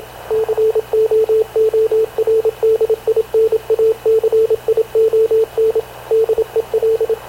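Morse code, heard as a keyed tone, about 27 wpm, 450 Hz, with radio noise. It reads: COORDINACION DEL
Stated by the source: Havana Radio CLA press (PX) broadcast, recorded by Radio Officer Paul Dumesnil aboard mv CNAV QUEST/CZDO.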